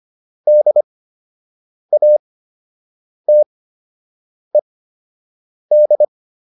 Morse code, heard as a keyed tone, about 25 wpm, spaced as for 7 wpm, 600 Hz. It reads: DATED